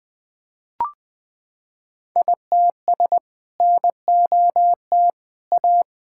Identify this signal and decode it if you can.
Morse code, heard as a keyed tone, about 20 wpm, 700 Hz. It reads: ITS NOT A
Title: Sentences with 2-Word Phrases - 20wpm